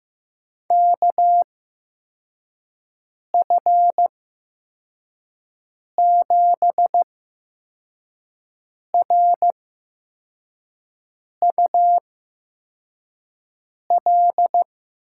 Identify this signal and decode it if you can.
Morse code, heard as a keyed tone, about 15 wpm, spaced as for 4 wpm, 700 Hz. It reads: KF7RUL
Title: USA Call Signs - 6 Characters - 8x Character Spacing - 15wpm